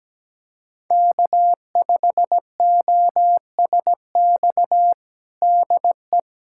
Morse code, heard as a keyed tone, about 17 wpm, 700 Hz. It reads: K5OSX DE
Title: USA Callsign DE Callsign - 5 Characters - 17wpm